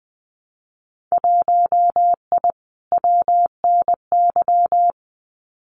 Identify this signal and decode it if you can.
Morse code, heard as a keyed tone, about 20 wpm, 700 Hz. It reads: 1I WNY
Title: Winter Field Day Exchanges Encore 20wpm